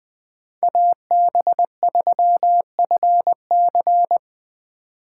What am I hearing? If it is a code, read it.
Morse code, heard as a keyed tone, about 20 wpm, 700 Hz. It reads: AB3FC
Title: Speed Racing - USA 2x2 Call Signs 17wpm